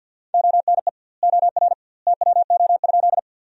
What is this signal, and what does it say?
Morse code, heard as a keyed tone, about 52 wpm, 700 Hz. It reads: ONE OR TWO?
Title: 2x Word-Spacing - Sentences from Top 100 Words - 50wpm